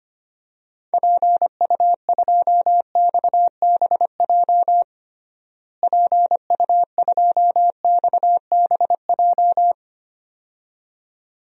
Morse code, heard as a keyed tone, about 25 wpm, 700 Hz. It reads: PU2XBJ PU2XBJ
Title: DXCC-100 Call Signs - 6 Characters - Repeated 2x - 25wpm